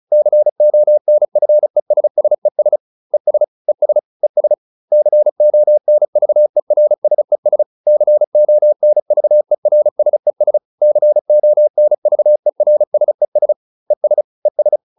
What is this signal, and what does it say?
Morse code, heard as a keyed tone, about 34 wpm, 600 Hz. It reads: CONFESSES ES ES ES CONVERSES CONVERSES CONVERSES ES ES